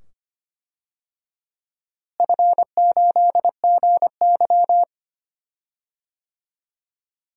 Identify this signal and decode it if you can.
Morse code, heard as a keyed tone, about 25 wpm, 700 Hz. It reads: F8GY
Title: DXCC-100 Call Signs - 4 Characters - Repeated 2x - 25wpm